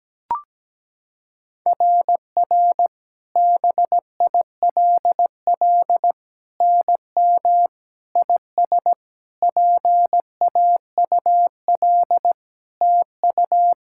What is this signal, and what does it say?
Morse code, heard as a keyed tone, about 17 wpm, 700 Hz. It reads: RR BILL NM IS PAUL TU
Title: SOTA QSO Practice 17wpm